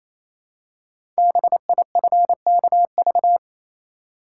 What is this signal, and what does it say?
Morse code, heard as a keyed tone, about 29 wpm, 700 Hz. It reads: BIFKV
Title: Code Groups - 5 Letter:Numbers - 28wpm